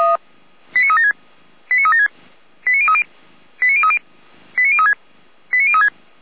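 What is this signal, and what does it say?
Five-tone selective call (ZVEI), heard as quick runs of five tones, tones of about 70 ms. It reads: 89376 89376 89030 89030 89036 89036
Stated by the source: ZVEI five-tone selective call received on an SDR.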